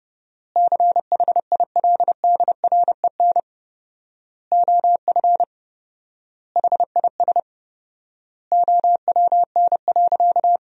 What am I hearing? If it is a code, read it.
Morse code, heard as a keyed tone, about 30 wpm, 700 Hz. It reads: CHILDREN OF HIS OWN.
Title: 4x Word-Spacing - Sentences from Top 300 Words - 30wpm